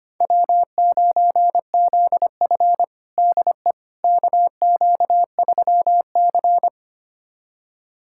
Morse code, heard as a keyed tone, about 25 wpm, 700 Hz. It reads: W9ZF DE KQ3C
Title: USA Callsign DE Callsign - 4 Characters - 25wpm